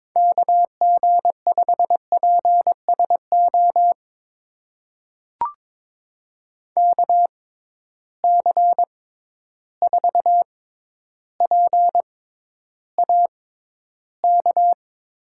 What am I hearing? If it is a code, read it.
Morse code, heard as a keyed tone, about 22 wpm, spaced as for 11 wpm, 700 Hz. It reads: KG5PSO KC4PAK
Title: USA Call Signs - 6 Characters - 6x Character Spacing - 22wpm